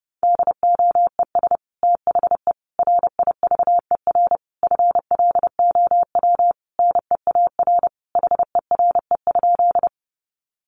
Morse code, heard as a keyed tone, about 28 wpm, 700 Hz. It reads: DOES THE RIVER FLOW NEAR HERE?
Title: Sentences from Top 1200 Words 30wpm